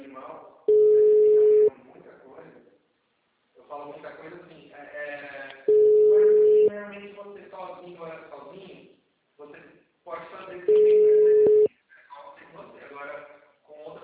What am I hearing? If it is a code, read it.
Ringback tone (Europe, 425 Hz), with speech.